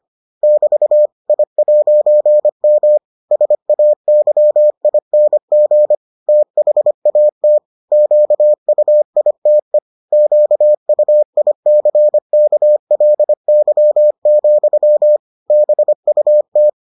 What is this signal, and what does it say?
Morse code, heard as a keyed tone, about 25 wpm, 600 Hz. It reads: <BT> I'M SAYING THAT QUITE QUICKLY, BUT